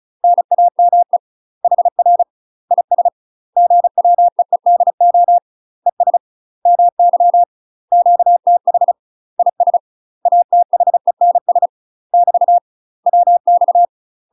Morse code, heard as a keyed tone, about 36 wpm, 700 Hz. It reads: NAME HR IS GWEEDO ES MY QTH IS ATHENS <BT> WX
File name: UK MASTER QSOs TRAINING / 35WPM / UK MASTER QSOs 0026